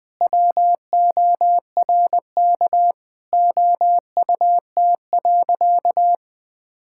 Morse code, heard as a keyed tone, about 21 wpm, 700 Hz. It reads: WORK OUT.
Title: All-Age Appropriate Jokes - 20wpm